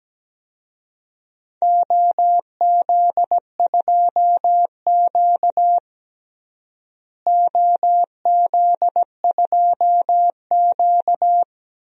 Morse code, heard as a keyed tone, about 17 wpm, 700 Hz. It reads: OZ2Q OZ2Q
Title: DXCC-100 Call Signs - 4 Characters - Repeated 3x - 17wpm